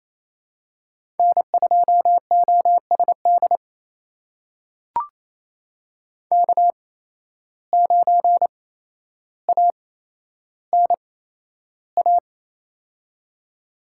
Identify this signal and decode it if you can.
Morse code, heard as a keyed tone, about 29 wpm, spaced as for 12 wpm, 700 Hz. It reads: N2OSD K9ANA
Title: USA Call Signs - 5 Characters - 8x Character Spacing - 28wpm